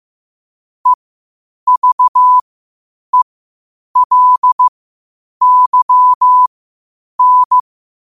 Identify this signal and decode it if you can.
Morse code, heard as a keyed tone, about 15 wpm, spaced as for 8 wpm, 1000 Hz. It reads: EVELYN
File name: qotdfiles / 8wpm